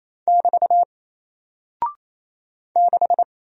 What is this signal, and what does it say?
Morse code, heard as a keyed tone, about 28 wpm, 700 Hz. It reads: <BT> 6
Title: Single Character (Letter:Number:Punctuation:Prosign) Alt 28wpm